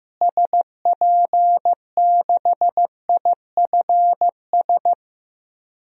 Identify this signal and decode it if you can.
Morse code, heard as a keyed tone, about 15 wpm, 700 Hz. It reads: SP6IFS
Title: DXCC-100 Call Signs - 6 Characters - Repeated 2x - 15wpm